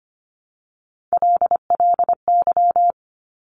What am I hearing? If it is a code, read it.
Morse code, heard as a keyed tone, about 25 wpm, 700 Hz. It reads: LLY
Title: Top Three Letter Combinations in Words 25wpm